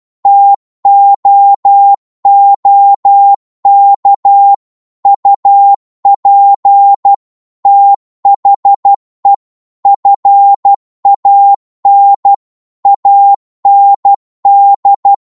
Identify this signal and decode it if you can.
Morse code, heard as a keyed tone, about 12 wpm, 800 Hz. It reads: TOOK UP THE FAN AND